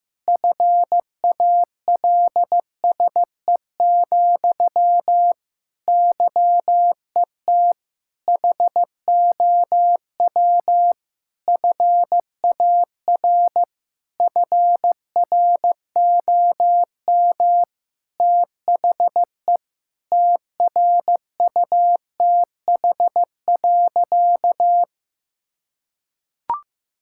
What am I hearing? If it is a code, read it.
Morse code, heard as a keyed tone, about 15 wpm, 700 Hz. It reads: FALSE, YET HOW FAR FROM THE TRUTH.
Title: Sentences from Top 1600 Words 15wpm